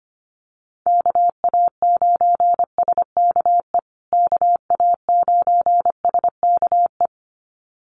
Morse code, heard as a keyed tone, about 25 wpm, 700 Hz. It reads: KA9SKE KA9SKE
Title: USA Call Signs Repeated 2x 25wpm